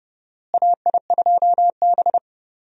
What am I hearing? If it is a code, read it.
Morse code, heard as a keyed tone, about 31 wpm, 700 Hz. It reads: AI2B
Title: USA Call Signs - 4 Characters - Repeated 2x - 30wpm